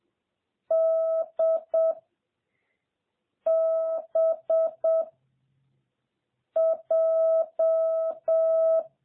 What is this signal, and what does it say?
Morse code, heard as a keyed tone, about 7 wpm, 650 Hz. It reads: D B J
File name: jh8cbh / 2021cw